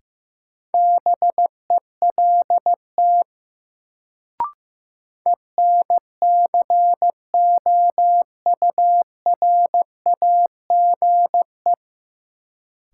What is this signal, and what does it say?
Morse code, heard as a keyed tone, about 15 wpm, 700 Hz. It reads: BELT ENCOURAGE
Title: Sentences from Top 1400 Words - New Word Drill - 15wpm